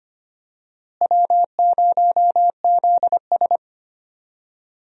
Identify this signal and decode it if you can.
Morse code, heard as a keyed tone, about 25 wpm, 700 Hz. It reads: W0ZS
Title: USA Call Signs - 4 Characters - Repeated 3x - 25wpm